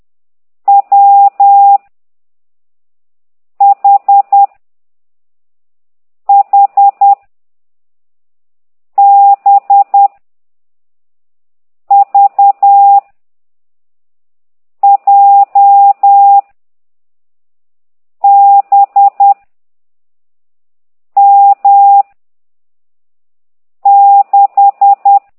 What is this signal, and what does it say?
Morse code, heard as a keyed tone, about 10 wpm, 800 Hz. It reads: W H H B V J B M 6
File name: ct2idw / exerc15